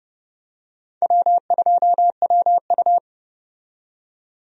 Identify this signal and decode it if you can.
Morse code, heard as a keyed tone, about 31 wpm, 700 Hz. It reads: W2WU
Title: USA Call Signs - 4 Characters - 30wpm